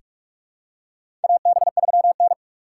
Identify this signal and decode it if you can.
Morse code, heard as a keyed tone, about 47 wpm, 700 Hz. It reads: AB3N
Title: USA Call Signs - 4 Characters - 4x Character Spacing - 45wpm